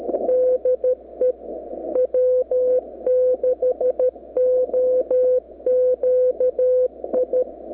Morse code, heard as a keyed tone, about 13 wpm, 510 Hz. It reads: DE W6OQI
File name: w6oqi 9-11-11-40M-SNIP